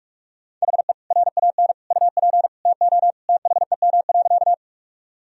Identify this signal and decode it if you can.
Morse code, heard as a keyed tone, about 46 wpm, 700 Hz. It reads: HE RAN UP TO THEM.
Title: Sentences from Top 1000 Words - Review Part 3 - 45wpm